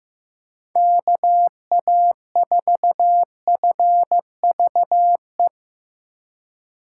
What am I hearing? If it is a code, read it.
Morse code, heard as a keyed tone, about 15 wpm, 700 Hz. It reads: KA4FVE
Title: USA Call Signs - 6 Characters - 2x Character Spacing - 15wpm